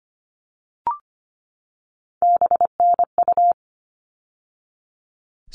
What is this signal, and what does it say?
Morse code, heard as a keyed tone, about 25 wpm, 700 Hz. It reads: BNU
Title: All Cut Numbers - 3-Digit 25wpm